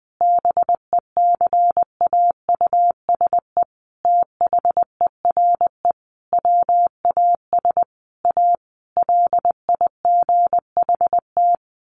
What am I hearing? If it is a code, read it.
Morse code, heard as a keyed tone, about 20 wpm, 700 Hz. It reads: BECAUSE THERE WAS A LIGHT
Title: Sentences from Top 1000 Words 20wpm